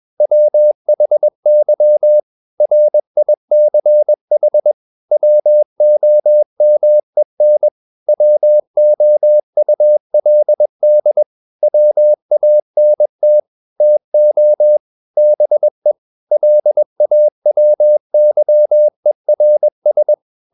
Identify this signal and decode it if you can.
Morse code, heard as a keyed tone, about 21 wpm, 600 Hz. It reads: WHY RICH WOMEN WOULD WANT TO BE LAWYERS
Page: CW med Gnister